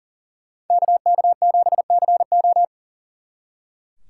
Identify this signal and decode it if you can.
Morse code, heard as a keyed tone, about 48 wpm, 700 Hz. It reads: KK7CO
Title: Speed Racing - USA 2x2 Call Signs 40wpm